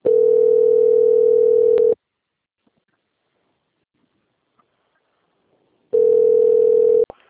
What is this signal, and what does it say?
Ringback tone (North America), 440 + 480 Hz.